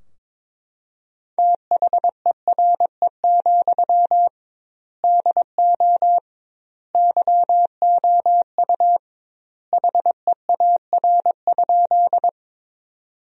Morse code, heard as a keyed tone, about 22 wpm, 700 Hz. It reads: THERE, DO YOU HEAR?
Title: 2x Word-Spacing - Sentences from Top 300 Words - 22wpm